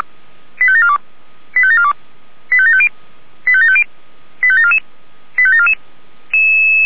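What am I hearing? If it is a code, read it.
Five-tone selective call (ZVEI), heard as quick runs of five tones, tones of about 70 ms. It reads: 85742 85742 85750 85750 85744 85744
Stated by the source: ZVEI five-tone call received by SDR, from the feuerwehr-orlamuende.org fire brigade audio archive.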